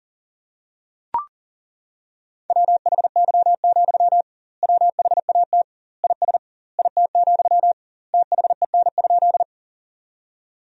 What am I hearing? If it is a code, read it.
Morse code, heard as a keyed tone, about 40 wpm, 700 Hz. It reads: WHY, WHAT IS IT, THEN?
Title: 2x Word-Spacing - Sentences from Top 200 Words - 40wpm